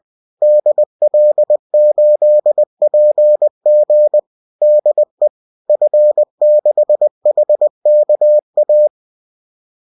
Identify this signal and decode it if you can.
Morse code, heard as a keyed tone, about 20 wpm, 600 Hz. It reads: DL8PG DE F6HKA